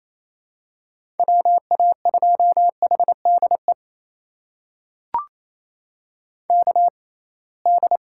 Morse code, heard as a keyed tone, about 28 wpm, spaced as for 21 wpm, 700 Hz. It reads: WA2HDE KD
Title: USA Call Signs - 6 Characters - 6x Character Spacing - 28wpm